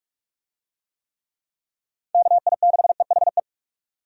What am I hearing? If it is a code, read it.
Morse code, heard as a keyed tone, about 46 wpm, 700 Hz. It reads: KI6EHE
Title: USA Call Signs - 6 Characters - 45wpm